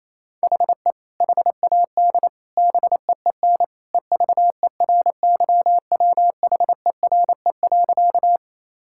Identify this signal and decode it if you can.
Morse code, heard as a keyed tone, about 28 wpm, 700 Hz. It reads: HE HAD BEEN EVERYWHERE.
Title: Sentences from Top 1600 Words 28wpm